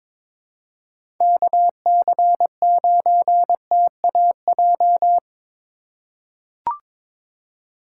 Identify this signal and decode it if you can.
Morse code, heard as a keyed tone, about 21 wpm, 700 Hz. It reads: KC9TAJ
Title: USA Call Signs - 6 Characters - 3x Character Spacing - 22wpm